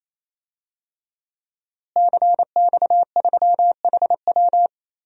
Morse code, heard as a keyed tone, about 28 wpm, 700 Hz. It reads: CX3HW
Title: DXCC-100 Call Signs - 5 Characters - Repeated 3x - 28wpm